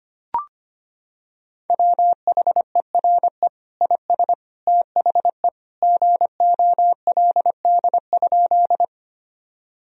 Morse code, heard as a keyed tone, about 25 wpm, 700 Hz. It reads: WHERE IS THE GOLD?
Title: Sentences from Top 1000 Words - Review Part 3 - 25wpm